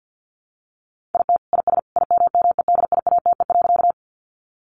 Morse code, heard as a keyed tone, about 50 wpm, 700 Hz. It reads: IT IS IMMEDIATE.
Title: Sentences from Top 1200 Words 50wpm